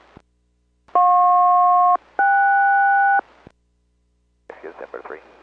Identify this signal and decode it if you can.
Aircraft SELCAL, heard as two pairs of tones, with speech. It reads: JP-KS